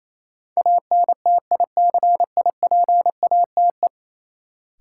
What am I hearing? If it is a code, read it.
Morse code, heard as a keyed tone, about 28 wpm, 700 Hz. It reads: ANTICIPATE
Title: Sentences from Top 1500 Words - New Word Drill Alt - 28wpm